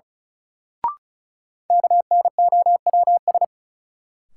Morse code, heard as a keyed tone, about 36 wpm, 700 Hz. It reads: KNOWS
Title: Sentences from Top 1000 Words - Review Part 1 - New Word Drill - 35wpm